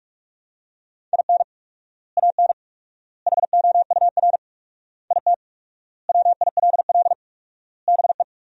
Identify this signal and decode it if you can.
Morse code, heard as a keyed tone, about 41 wpm, 700 Hz. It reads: IN AN HOUR IT WILL BE
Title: 4x Word-Spacing - Sentences from Top 400 Words - 45wpm